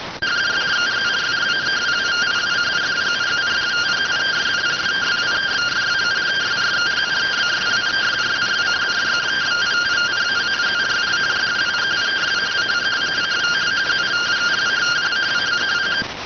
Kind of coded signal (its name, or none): two-tone teleprinter signal